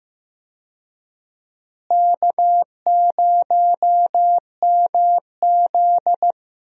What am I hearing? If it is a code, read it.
Morse code, heard as a keyed tone, about 15 wpm, 700 Hz. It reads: K0MZ
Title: USA Call Signs - 4 Characters - Repeated 3x - 15wpm